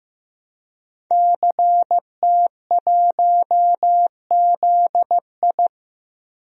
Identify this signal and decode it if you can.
Morse code, heard as a keyed tone, about 15 wpm, 700 Hz. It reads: CT1ZI
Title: DXCC-100 Call Signs - 5 Characters - Repeated 2x - 15wpm